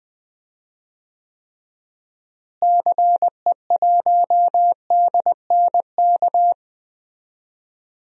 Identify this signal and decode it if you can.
Morse code, heard as a keyed tone, about 20 wpm, 700 Hz. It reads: CE1DNK
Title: DXCC-100 Call Signs - 6 Characters - Repeated 2x - 20wpm